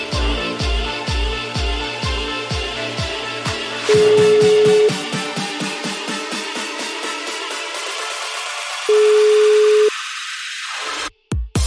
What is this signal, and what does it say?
Ringback tone (Europe, 425 Hz), with noise.